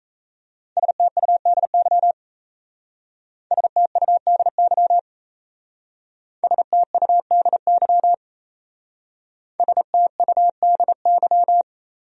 Morse code, heard as a keyed tone, about 36 wpm, 700 Hz. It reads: STUDY STUDY STUDY STUDY